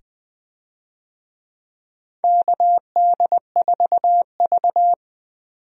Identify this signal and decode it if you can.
Morse code, heard as a keyed tone, about 20 wpm, 700 Hz. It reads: KD4V
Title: USA Call Signs - 4 Characters - 20wpm